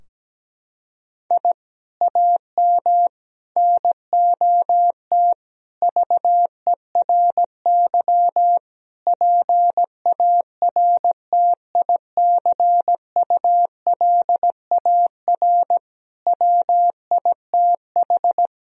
Morse code, heard as a keyed tone, about 17 wpm, 700 Hz. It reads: I AM NOT VERY PARTICULAR WITH
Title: Sentences from Top 1000 Words - Review Part 4 - 17wpm